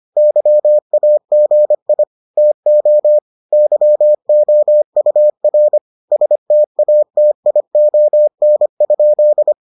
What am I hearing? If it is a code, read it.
Morse code, heard as a keyed tone, about 25 wpm, 600 Hz. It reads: YAGI TO YOUR STATION?